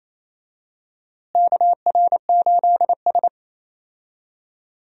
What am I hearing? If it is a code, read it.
Morse code, heard as a keyed tone, about 28 wpm, 700 Hz. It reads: KR8S
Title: USA Call Signs - 4 Characters - Repeated 3x - 28wpm